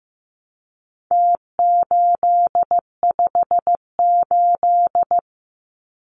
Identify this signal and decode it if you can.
Morse code, heard as a keyed tone, about 15 wpm, 700 Hz. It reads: T858
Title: Numbers - Contest Style - 4-Digit 15wpm